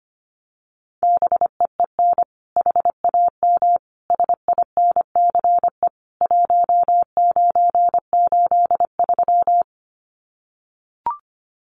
Morse code, heard as a keyed tone, about 25 wpm, 700 Hz. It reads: BEEN HAM SINCE 1983
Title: Common QSO Phrases 25wpm